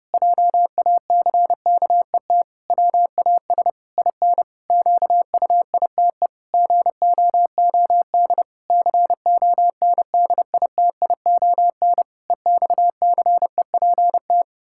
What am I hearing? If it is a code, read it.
Morse code, heard as a keyed tone, about 28 wpm, 700 Hz. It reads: JACKET WAS IN QUITE GOOD CONDITION EXCEPT
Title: Pangram Challenge v2 - 30wpm